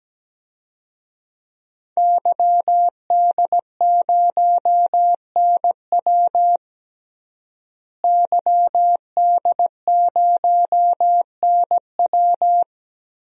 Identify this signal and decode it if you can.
Morse code, heard as a keyed tone, about 17 wpm, 700 Hz. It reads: YD0NW YD0NW
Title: DXCC-100 Call Signs - 5 Characters - Repeated 3x - 17wpm